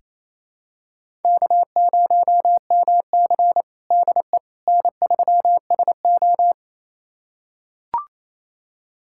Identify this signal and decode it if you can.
Morse code, heard as a keyed tone, about 28 wpm, 700 Hz. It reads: K0MC DE N3SO
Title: USA Callsign DE Callsign - 4 Characters - 28wpm